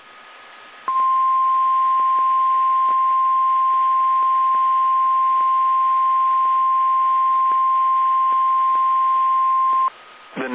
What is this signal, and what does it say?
Emergency Alert System (SAME) tones (the 1050 Hz attention tone), with speech. It is heard from 0.8 to 9.9 s.
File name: eas / KSDZKDJL / KSDZKDJL